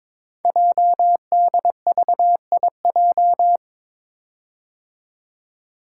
Morse code, heard as a keyed tone, about 22 wpm, 700 Hz. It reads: JDVIJ
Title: Code Groups - 5 Letter:Numbers - 22wpm